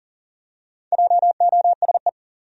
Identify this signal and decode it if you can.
Morse code, heard as a keyed tone, about 40 wpm, 700 Hz. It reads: JOSE
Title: USA Common Names - Max 4 Letters - 40wpm